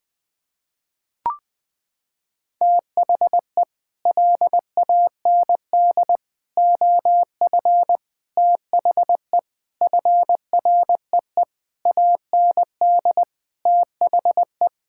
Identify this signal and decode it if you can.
Morse code, heard as a keyed tone, about 20 wpm, 700 Hz. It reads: THE LAND OF THE FREE AND THE
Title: Sets of 9 Words - 20wpm